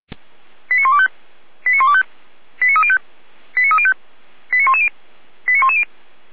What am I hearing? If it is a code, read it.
Five-tone selective call (ZVEI), heard as quick runs of five tones, tones of about 70 ms. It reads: 89125 89125 89305 89305 89110 89110